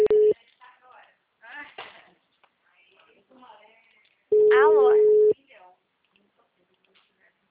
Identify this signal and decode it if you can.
Ringback tone (Europe, 425 Hz), with speech.